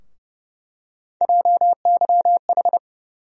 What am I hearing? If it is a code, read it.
Morse code, heard as a keyed tone, about 30 wpm, 700 Hz. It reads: JYH
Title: USA Call Sign Postfixes - 3 Letters 30wpm